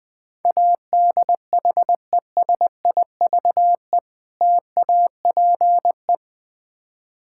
Morse code, heard as a keyed tone, about 22 wpm, 700 Hz. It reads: ADHESIVE TAPE
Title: Difficult Words - Speed-Racing with Hints - 20wpm